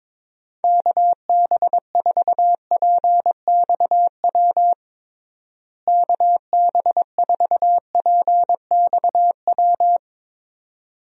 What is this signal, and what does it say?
Morse code, heard as a keyed tone, about 22 wpm, 700 Hz. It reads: KB4PXW KB4PXW
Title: USA Call Signs - 6 Characters - Repeated 2x - 22wpm